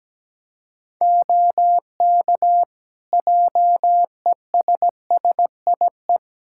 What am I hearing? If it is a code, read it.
Morse code, heard as a keyed tone, about 17 wpm, 700 Hz. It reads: OK JESSIE
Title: Practice QSO - Exchange 4 - Ending - 17wpm